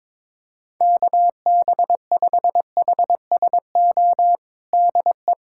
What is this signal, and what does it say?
Morse code, heard as a keyed tone, about 22 wpm, 700 Hz. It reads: KB5HSO DE
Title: USA Callsign DE Callsign 22wpm